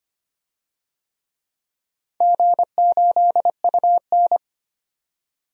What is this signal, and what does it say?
Morse code, heard as a keyed tone, about 25 wpm, 700 Hz. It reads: G8UN
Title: DXCC-100 Call Signs - 4 Characters - 25wpm